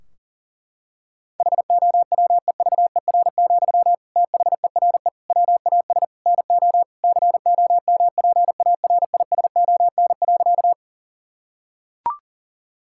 Morse code, heard as a keyed tone, about 40 wpm, 700 Hz. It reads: HOWEVER, THERE WAS NO COMPARISON.
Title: Sentences from Top 1600 Words 40wpm